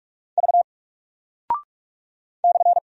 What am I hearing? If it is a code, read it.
Morse code, heard as a keyed tone, about 44 wpm, 700 Hz. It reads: V /
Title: Single Character (Letter:Number:Punctuation:Prosign) Alt 45wpm